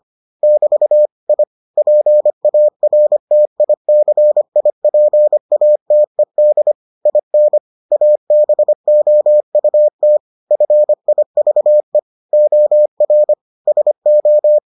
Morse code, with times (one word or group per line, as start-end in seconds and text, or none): 0.4-1.1 <BT>
1.3-1.4 I
1.8-6.7 PARTICIPATED
7.1-7.6 IN
7.9-10.2 ABOUT
10.5-12.0 FIVE
12.3-13.3 OR
13.7-14.6 SO